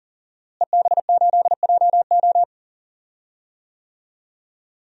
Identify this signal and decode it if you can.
Morse code, heard as a keyed tone, about 40 wpm, 700 Hz. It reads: EB8JO
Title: DXCC-100 Call Signs - 5 Characters - Repeated 2x - 40wpm